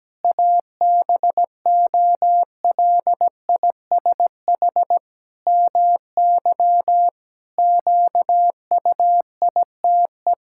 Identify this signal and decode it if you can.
Morse code, heard as a keyed tone, about 17 wpm, 700 Hz. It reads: ABOLISH MY QUITE
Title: Pangram Challenge v2 - 17wpm